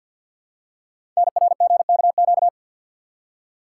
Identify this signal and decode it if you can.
Morse code, heard as a keyed tone, about 50 wpm, 700 Hz. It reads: NRGKQ
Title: Code Groups - 5 Letters - 50wpm